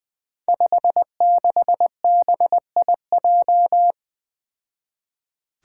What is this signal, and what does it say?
Morse code, heard as a keyed tone, about 20 wpm, 700 Hz. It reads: 56BIJ
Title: Five Challenging Letter-Number Combinations 20wpm